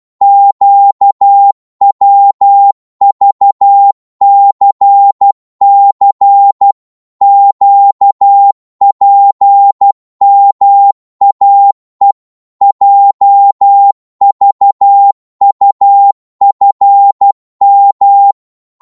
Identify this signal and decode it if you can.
Morse code, heard as a keyed tone, about 12 wpm, 800 Hz. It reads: QWVCC QPMAE JVUFM
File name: Morse / ltr-12wpm